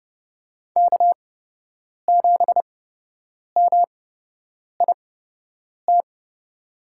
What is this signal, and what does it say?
Morse code, heard as a keyed tone, about 30 wpm, spaced as for 8 wpm, 700 Hz. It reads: K7MIT
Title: USA Call Signs - 5 Characters - 8x Character Spacing - 30wpm